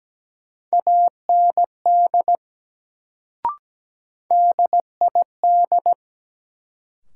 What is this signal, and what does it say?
Morse code, heard as a keyed tone, about 17 wpm, 700 Hz. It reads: AND DID
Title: Sets of 2 Words - New Word Drill - 17wpm